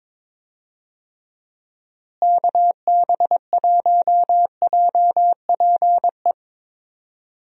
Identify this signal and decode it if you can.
Morse code, heard as a keyed tone, about 22 wpm, 700 Hz. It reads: KB1JPE
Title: USA Call Signs - 6 Characters - Repeated 3x - 22wpm